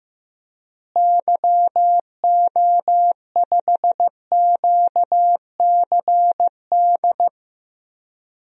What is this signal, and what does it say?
Morse code, heard as a keyed tone, about 15 wpm, 700 Hz. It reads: YO5QCD
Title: DXCC-100 Call Signs - 6 Characters - Repeated 2x - 15wpm